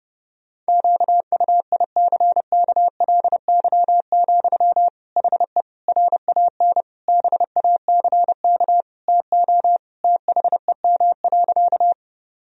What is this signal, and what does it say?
Morse code, heard as a keyed tone, about 32 wpm, 700 Hz. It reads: QUICKLY, HE RAN BACK TO THEM.
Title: Sentences from Top 1000 Words - Review Part 3 - 30wpm